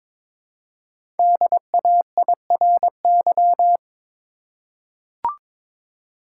Morse code, heard as a keyed tone, about 23 wpm, 700 Hz. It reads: DAIRY
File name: 5 Letter Words - First 3 Letters Given - 22wpm